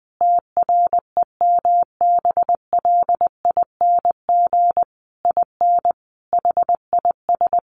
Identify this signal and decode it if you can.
Morse code, heard as a keyed tone, about 20 wpm, 700 Hz. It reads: TREMBLING IN HIS
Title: A Sound Of Thunder 20wpm